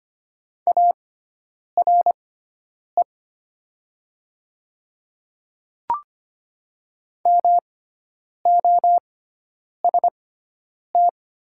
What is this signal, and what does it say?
Morse code, heard as a keyed tone, about 25 wpm, spaced as for 9 wpm, 700 Hz. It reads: ARE MOST
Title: 6x Character Spacing - Top 100 Words - 25wpm